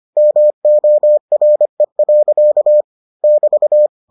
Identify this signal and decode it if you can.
Morse code, heard as a keyed tone, about 25 wpm, 600 Hz. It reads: MORE. <BT>